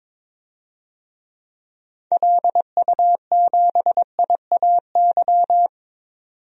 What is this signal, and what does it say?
Morse code, heard as a keyed tone, about 22 wpm, 700 Hz. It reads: LU7IAY
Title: DXCC-100 Call Signs - 6 Characters - 22wpm